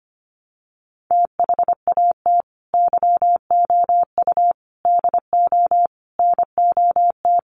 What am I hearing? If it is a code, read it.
Morse code, heard as a keyed tone, about 25 wpm, 700 Hz. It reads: THAT YOU DO NOT
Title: Sets of 5 Words Encore 25wpm